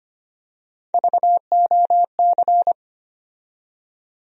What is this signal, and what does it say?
Morse code, heard as a keyed tone, about 25 wpm, 700 Hz. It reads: VOC
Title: USA Call Sign Postfixes - 3 Letters 25wpm